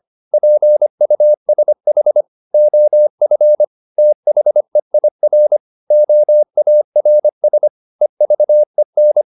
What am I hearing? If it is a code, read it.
Morse code, heard as a keyed tone, about 25 wpm, 600 Hz. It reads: PUSH OF THEIR OARS EVEN
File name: the old man and the sea part 1 25wpm